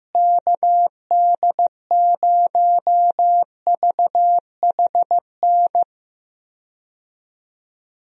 Morse code, heard as a keyed tone, about 15 wpm, 700 Hz. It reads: KD0VHN
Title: USA Call Signs - 6 Characters - Repeated 2x - 15wpm